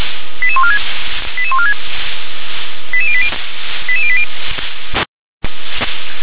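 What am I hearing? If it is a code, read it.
Five-tone selective call (ZVEI), heard as quick runs of five tones, tones of about 80 ms. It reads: 88146 88146 88080 88080
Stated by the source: Five-tone ZVEI selective call 88910 received by SDR.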